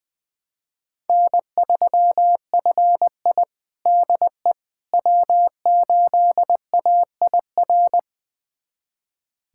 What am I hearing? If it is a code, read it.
Morse code, heard as a keyed tone, about 20 wpm, 700 Hz. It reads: N3FI DE W8AIR
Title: USA Callsign DE Callsign 20wpm